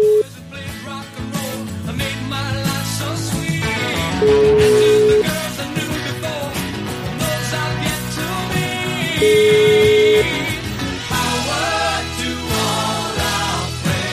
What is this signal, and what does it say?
Ringback tone (Europe, 425 Hz), with noise.